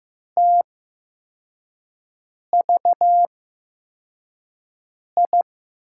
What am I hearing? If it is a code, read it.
Morse code, heard as a keyed tone, about 15 wpm, spaced as for 4 wpm, 700 Hz. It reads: TVI
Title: USA Call Signs - 6 Characters - 8x Character Spacing - 15wpm